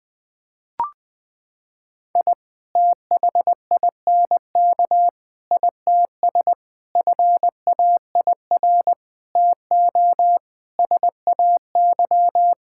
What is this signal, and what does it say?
Morse code, heard as a keyed tone, about 20 wpm, 700 Hz. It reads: I THINK ITS FAIR TO SAY
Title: Sentences with 6-Word Phrases - 20wpm